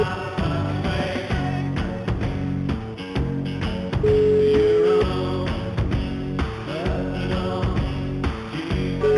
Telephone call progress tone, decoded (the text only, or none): ringback tone